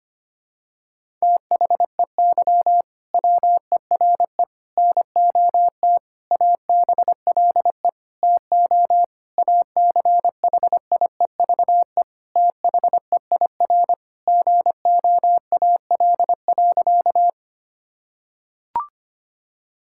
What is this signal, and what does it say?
Morse code, heard as a keyed tone, about 25 wpm, 700 Hz. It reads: THEY WERE NOT ABLE TO ACHIEVE THEIR GOAL.
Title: Sentences from Top 1000 Words - Review Part 6 - 25wpm